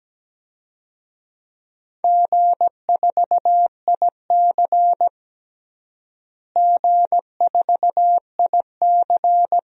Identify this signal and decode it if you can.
Morse code, heard as a keyed tone, about 17 wpm, 700 Hz. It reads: G4IC G4IC
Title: DXCC-100 Call Signs - 4 Characters - Repeated 2x - 17wpm